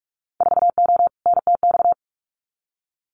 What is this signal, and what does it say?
Morse code, heard as a keyed tone, about 44 wpm, 700 Hz. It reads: 4O NTX
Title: Winter Field Day Exchanges 45wpm